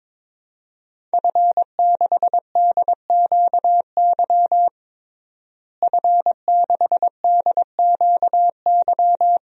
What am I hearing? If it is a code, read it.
Morse code, heard as a keyed tone, about 22 wpm, 700 Hz. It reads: F6DQY F6DQY
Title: DXCC-100 Call Signs - 5 Characters - Repeated 3x - 22wpm